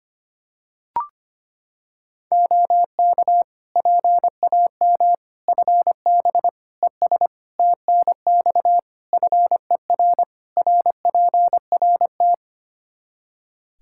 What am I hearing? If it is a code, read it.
Morse code, heard as a keyed tone, about 25 wpm, 700 Hz. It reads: OK PAM FB ES TNX FER RPRT
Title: K1USN SST QSO Practice 25wpm